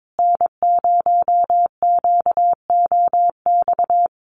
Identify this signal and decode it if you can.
Morse code, heard as a keyed tone, about 22 wpm, 700 Hz. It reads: N0QOX
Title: USA Call Signs Repeated 3x 22wpm